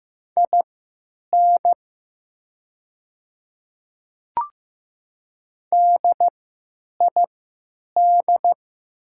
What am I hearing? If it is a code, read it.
Morse code, heard as a keyed tone, about 15 wpm, spaced as for 9 wpm, 700 Hz. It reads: IN DID
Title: 3x Character Spacing - Top 100 Words - 15wpm